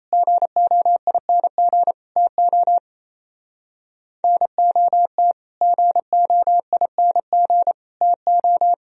Morse code, heard as a keyed tone, about 33 wpm, 700 Hz. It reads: GOING TO NOT GOING TO
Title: Speed-Racing - Sets of 3 Words 28wpm